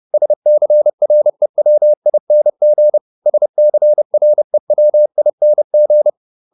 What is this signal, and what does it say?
Morse code, heard as a keyed tone, about 30 wpm, 600 Hz. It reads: SCREWING SCREWING
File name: ING8 30